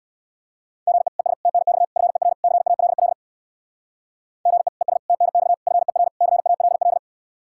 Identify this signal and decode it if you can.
Morse code, heard as a keyed tone, about 63 wpm, 700 Hz. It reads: GE ES TNX FER CALL GE ES TNX FER CALL